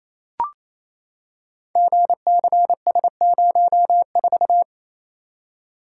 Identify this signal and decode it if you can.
Morse code, heard as a keyed tone, about 28 wpm, 690 Hz. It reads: GCS04
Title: Code Groups - 5 Letter:Numbers - 28wpm